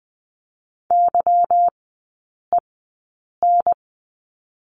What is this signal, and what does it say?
Morse code, heard as a keyed tone, about 20 wpm, spaced as for 8 wpm, 700 Hz. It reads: YEN
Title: 178 Lesson 29e - Slash - Three Characters 20wpm